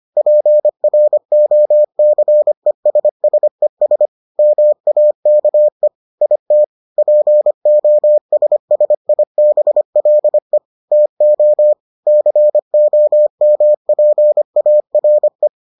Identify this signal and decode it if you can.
Morse code, heard as a keyed tone, about 25 wpm, 600 Hz. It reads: PROCESSES MAKE IT POSSIBLE TO COMPARE